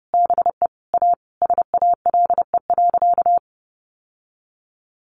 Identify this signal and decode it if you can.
Morse code, heard as a keyed tone, about 30 wpm, 700 Hz. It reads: BE A SALE.
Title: Sentences from Top 1400 Words 30wpm